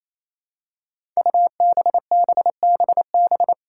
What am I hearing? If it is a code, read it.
Morse code, heard as a keyed tone, about 28 wpm, 700 Hz. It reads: UBBBB
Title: Five Challenging Letter-Number Combinations 28wpm